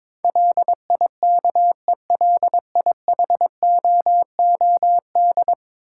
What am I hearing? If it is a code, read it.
Morse code, heard as a keyed tone, about 22 wpm, 700 Hz. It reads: LIKELIHOOD
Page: morse-code-ninja-practice-22wpm